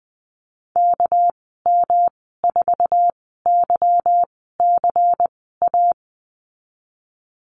Morse code, heard as a keyed tone, about 20 wpm, spaced as for 15 wpm, 700 Hz. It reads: KM4YCA
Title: 207 Lesson 33g - Letter j - Call Signs 20wpm